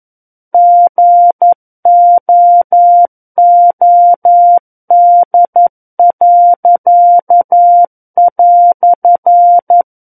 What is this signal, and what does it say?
Morse code, heard as a keyed tone, about 11 wpm, 700 Hz. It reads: GOOD."
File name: 11 WPM